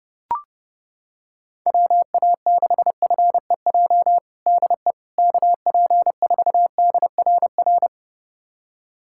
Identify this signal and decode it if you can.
Morse code, heard as a keyed tone, about 30 wpm, 700 Hz. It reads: WA6FEJ DE KP4DRR
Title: USA Callsign DE Callsign - 6 Characters - 30wpm